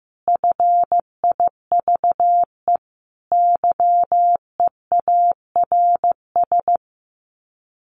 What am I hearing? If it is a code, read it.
Morse code, heard as a keyed tone, about 15 wpm, 700 Hz. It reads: FIVE YEARS
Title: Sets of 5 Words Encore 15wpm